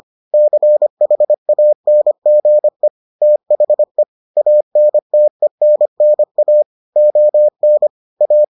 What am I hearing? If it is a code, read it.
Morse code, heard as a keyed tone, about 25 wpm, 600 Hz. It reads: CHANGE THE ANTENNA ON A